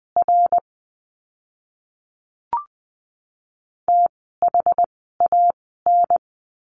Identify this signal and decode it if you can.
Morse code, heard as a keyed tone, about 20 wpm, spaced as for 14 wpm, 700 Hz. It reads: R THAN
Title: 061 Lesson 11f - Letter h - Words 20wpm